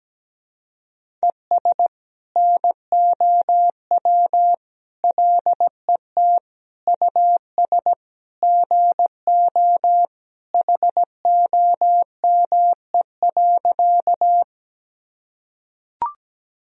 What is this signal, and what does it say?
Morse code, heard as a keyed tone, about 17 wpm, 700 Hz. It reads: ES NOW LET US GO HOME.
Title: Sentences from Top 300 Words and Abbreviations 17wpm